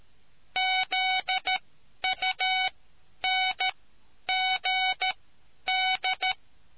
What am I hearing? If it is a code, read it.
Morse code, heard as a keyed tone, about 13 wpm, spaced as for 10 wpm, 750 Hz. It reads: ZUNGD